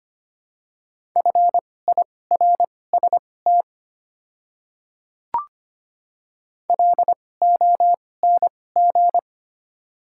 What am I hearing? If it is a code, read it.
Morse code, heard as a keyed tone, about 25 wpm, spaced as for 18 wpm, 700 Hz. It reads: FIRST LONG
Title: 2x Character Spacing - Top 100 Words - 25wpm